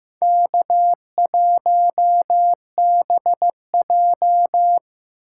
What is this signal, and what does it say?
Morse code, heard as a keyed tone, about 15 wpm, 700 Hz. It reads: K1BJ
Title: USA Call Signs - 4 Characters - 3x Character Spacing - 15wpm